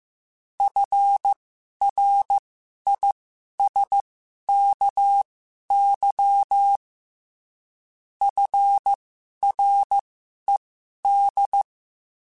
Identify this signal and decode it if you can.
Morse code, heard as a keyed tone, about 15 wpm, spaced as for 11 wpm, 800 Hz. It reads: FRISKY FRED
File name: F Runs / F Runs mp3s / F Runs @13